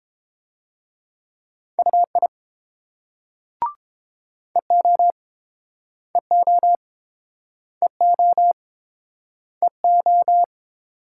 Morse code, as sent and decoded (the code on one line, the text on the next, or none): ..- .. / . --- / . --- / . --- / . ---
UI EO EO EO EO